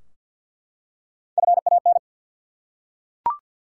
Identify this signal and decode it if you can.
Morse code, heard as a keyed tone, about 49 wpm, 700 Hz. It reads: FAN
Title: Three Letter Words - 50wpm